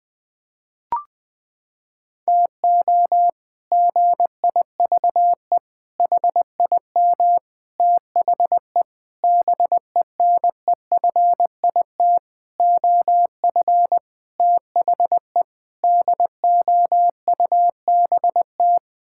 Morse code, heard as a keyed tone, about 20 wpm, 700 Hz. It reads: TO GIVE HIM THE BENEFIT OF THE DOUBT
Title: Sets of 8 Words - 20wpm